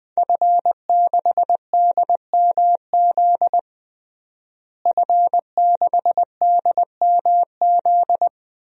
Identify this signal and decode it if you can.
Morse code, heard as a keyed tone, about 20 wpm, 700 Hz. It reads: F6DMZ F6DMZ
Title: DXCC-100 Call Signs - 5 Characters - Repeated 2x - 20wpm